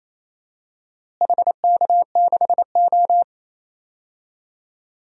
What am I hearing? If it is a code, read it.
Morse code, heard as a keyed tone, about 28 wpm, 700 Hz. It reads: HK6O
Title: DXCC-100 Call Signs - 4 Characters - Repeated 3x - 28wpm